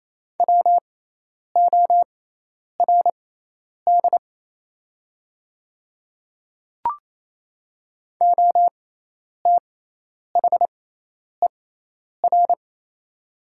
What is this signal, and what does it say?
Morse code, heard as a keyed tone, about 27 wpm, spaced as for 10 wpm, 700 Hz. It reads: WORD OTHER
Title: 6x Character Spacing - Top 200 Words - 28wpm